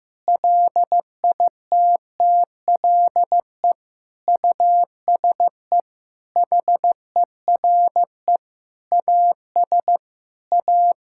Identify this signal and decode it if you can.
Morse code, heard as a keyed tone, about 15 wpm, 700 Hz. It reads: LITTLE USE HERE AS A
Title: Sentences from Top 1000 Words - Review Part 5 - 15wpm